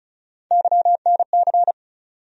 Morse code, heard as a keyed tone, about 36 wpm, 700 Hz. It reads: YNC
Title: USA Call Sign Postfixes - 3 Letters 35wpm